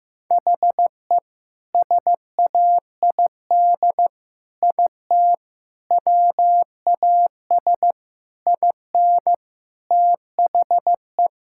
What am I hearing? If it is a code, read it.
Morse code, heard as a keyed tone, about 15 wpm, 700 Hz. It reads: HE SAID IT WAS IN THE
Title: Sentences from Top 1600 Words 15wpm